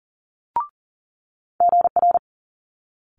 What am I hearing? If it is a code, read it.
Morse code, heard as a keyed tone, about 42 wpm, 700 Hz. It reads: GR
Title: Multi-letter Phonemes Quick Fire 40wpm